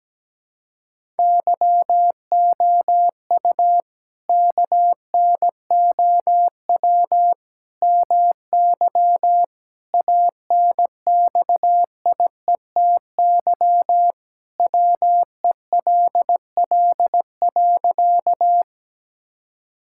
Morse code, heard as a keyed tone, about 17 wpm, 700 Hz. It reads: YOU KNOW MY ANXIETY WELL.
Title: Sentences from Top 1600 Words 17wpm